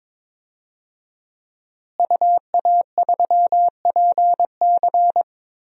Morse code, heard as a keyed tone, about 22 wpm, 700 Hz. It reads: UA3PC
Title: DXCC-100 Call Signs - 5 Characters - Repeated 3x - 22wpm